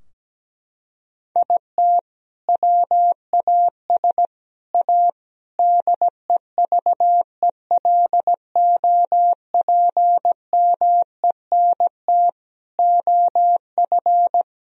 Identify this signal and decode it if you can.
Morse code, heard as a keyed tone, about 17 wpm, 700 Hz. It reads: IT WAS A DEVELOPMENT OF
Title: Sentences from Top 1000 Words - Review Part 4 - 17wpm